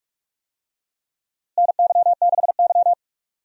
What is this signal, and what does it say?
Morse code, heard as a keyed tone, about 46 wpm, 700 Hz. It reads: NY6Y
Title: USA Call Signs - 4 Characters - Repeated 3x - 45wpm